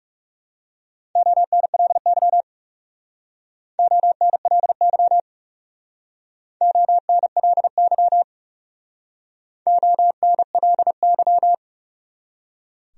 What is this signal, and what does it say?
Morse code, heard as a keyed tone, about 40 wpm, 700 Hz. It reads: ONLY ONLY ONLY ONLY